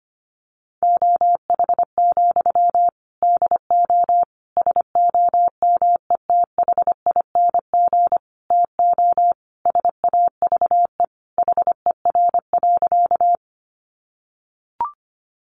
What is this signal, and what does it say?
Morse code, heard as a keyed tone, about 25 wpm, 700 Hz. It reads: OH, DO SOMETHING TO SAVE HER.
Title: Sentences from Top 1100 Words 25wpm